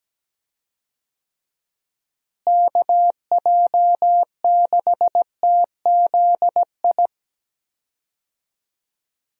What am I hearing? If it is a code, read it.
Morse code, heard as a keyed tone, about 17 wpm, 700 Hz. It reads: KJ6TZI
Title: USA Call Signs - 6 Characters - Repeated 2x - 17wpm